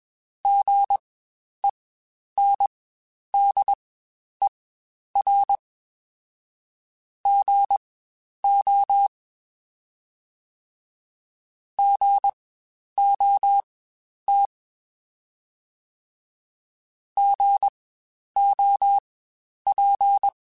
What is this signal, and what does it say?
Morse code, heard as a keyed tone, about 20 wpm, spaced as for 10 wpm, 800 Hz. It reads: GENDER GO GOT GOP
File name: Spaced Alphas / Spaced Alphas mp3s / Gs sp @20